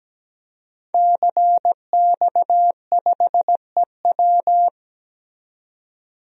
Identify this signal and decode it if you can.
Morse code, heard as a keyed tone, about 17 wpm, 700 Hz. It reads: CX5EW
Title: DXCC-100 Call Signs - 5 Characters - 17wpm